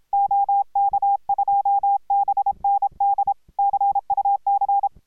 Morse code, heard as a keyed tone, about 26 wpm, 810 Hz. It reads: OK2BND CUC